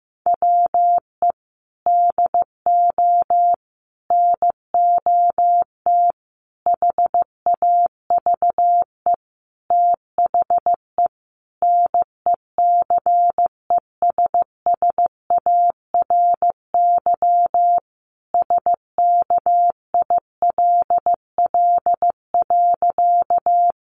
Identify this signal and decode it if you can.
Morse code, heard as a keyed tone, about 15 wpm, 700 Hz. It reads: WE DO NOT HAVE THE NECESSARY SKILL.
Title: Sentences from Top 1200 Words 15wpm